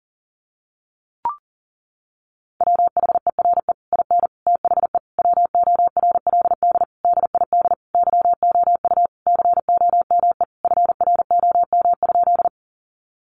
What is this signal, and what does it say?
Morse code, heard as a keyed tone, about 40 wpm, 700 Hz. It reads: WHERE IN THE WORLD DID YOU COME FROM?
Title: Sentences from Top 200 Words 40wpm